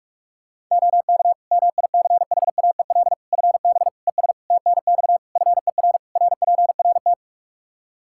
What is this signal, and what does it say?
Morse code, heard as a keyed tone, about 46 wpm, 700 Hz. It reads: OK MICHAEL FB ES TNX FER RPRT
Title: K1USN SST QSO Practice 45wpm